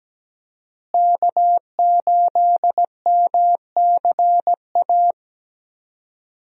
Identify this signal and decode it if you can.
Morse code, heard as a keyed tone, about 17 wpm, 700 Hz. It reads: K8MCA
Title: USA Call Signs - 5 Characters - Repeated 2x - 17wpm